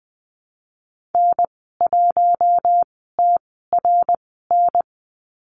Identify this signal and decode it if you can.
Morse code, heard as a keyed tone, about 20 wpm, spaced as for 15 wpm, 700 Hz. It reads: N1TRN
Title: 128 Lesson 21g - Number 6 - Call Signs 20wpm